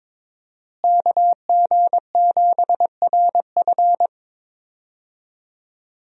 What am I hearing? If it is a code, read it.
Morse code, heard as a keyed tone, about 22 wpm, 700 Hz. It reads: KG7RF
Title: USA Call Signs - 5 Characters - Repeated 3x - 22wpm